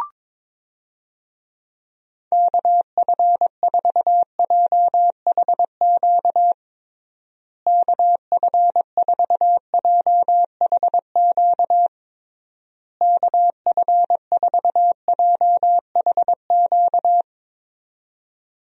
Morse code, heard as a keyed tone, about 22 wpm, 700 Hz. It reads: KF4JHQ KF4JHQ KF4JHQ
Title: USA Call Signs - 6 Characters - Repeated 3x - 22wpm